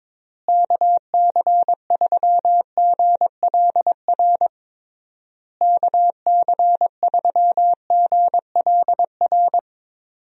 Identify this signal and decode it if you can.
Morse code, heard as a keyed tone, about 22 wpm, 700 Hz. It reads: KC3GLR KC3GLR
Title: USA Call Signs - 6 Characters - Repeated 3x - 22wpm